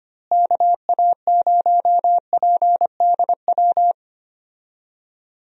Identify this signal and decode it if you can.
Morse code, heard as a keyed tone, about 25 wpm, 700 Hz. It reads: KA0PDW
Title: USA Call Signs - 6 Characters - Repeated 3x - 25wpm